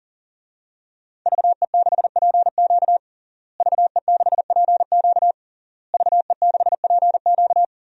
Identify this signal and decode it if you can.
Morse code, heard as a keyed tone, about 40 wpm, 700 Hz. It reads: VE6PQ VE6PQ VE6PQ
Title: DXCC-100 Call Signs - 5 Characters - Repeated 3x - 40wpm